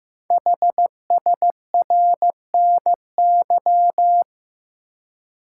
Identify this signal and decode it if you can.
Morse code, heard as a keyed tone, about 15 wpm, 700 Hz. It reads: HSRNY
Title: Code Groups - 5 Letters - 15wpm